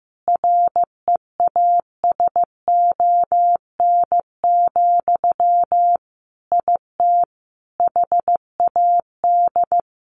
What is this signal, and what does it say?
Morse code, heard as a keyed tone, about 15 wpm, 700 Hz. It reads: REASON, IT HAD
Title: Sentences from Top 1500 Words 15wpm